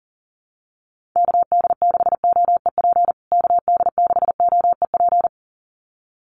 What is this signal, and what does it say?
Morse code, heard as a keyed tone, about 40 wpm, 700 Hz. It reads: KD6OEP KD6OEP
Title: USA Call Signs Repeated 2x 40wpm